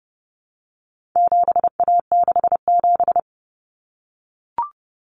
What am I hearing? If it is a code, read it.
Morse code, heard as a keyed tone, about 30 wpm, 700 Hz. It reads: 7A67
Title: Common Cut Numbers - 4-Digit 30wpm